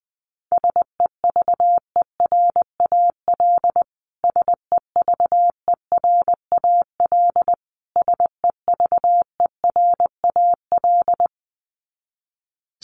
Morse code, heard as a keyed tone, about 19 wpm, 700 Hz. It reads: SEVERAL SEVERAL SEVERAL
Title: Top 500 Words Repeated 20wpm